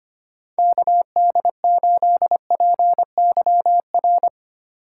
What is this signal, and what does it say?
Morse code, heard as a keyed tone, about 32 wpm, 700 Hz. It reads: KD8PYR